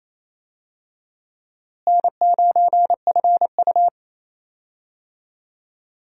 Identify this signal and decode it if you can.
Morse code, heard as a keyed tone, about 28 wpm, 700 Hz. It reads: N9FU
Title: USA Call Signs - 4 Characters - Repeated 3x - 28wpm